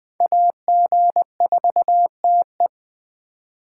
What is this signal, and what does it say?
Morse code, heard as a keyed tone, about 21 wpm, 700 Hz. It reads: AG4TE
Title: Speed Racing - USA 2x2 Call Signs 17wpm